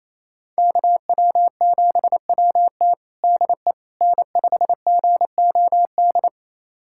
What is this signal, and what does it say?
Morse code, heard as a keyed tone, about 28 wpm, 700 Hz. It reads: KW7WT DE N5GOD
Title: USA Callsign DE Callsign - 5 Characters - 28wpm